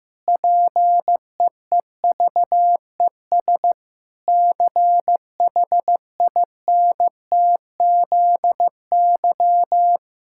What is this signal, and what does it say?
Morse code, heard as a keyed tone, about 14 wpm, 700 Hz. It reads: PEEVES CHINTZY
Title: Pangram Challenge v2 - 15wpm